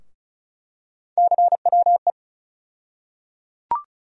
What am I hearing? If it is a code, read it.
Morse code, heard as a keyed tone, about 34 wpm, 700 Hz. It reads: CWE
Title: USA Call Sign Postfixes - 3 Letters 35wpm